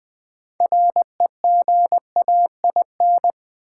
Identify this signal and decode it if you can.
Morse code, heard as a keyed tone, about 20 wpm, 700 Hz. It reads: REGAIN
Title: 6 Letter Words - First 3 Letters Given - 20wpm